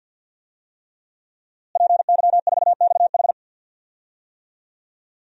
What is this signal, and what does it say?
Morse code, heard as a keyed tone, about 51 wpm, 700 Hz. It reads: PY4XH
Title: DXCC-100 Call Signs - 5 Characters - Repeated 3x - 50wpm